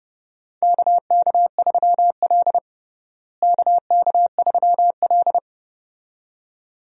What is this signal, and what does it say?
Morse code, heard as a keyed tone, about 29 wpm, 700 Hz. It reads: KK3L KK3L
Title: USA Call Signs - 4 Characters - Repeated 2x - 30wpm